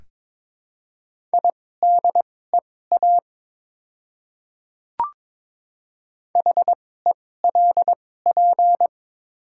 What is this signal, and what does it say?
Morse code, heard as a keyed tone, about 22 wpm, spaced as for 16 wpm, 700 Hz. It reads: IDEA HELP
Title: 2x Character Spacing - Top 500 Words - 22wpm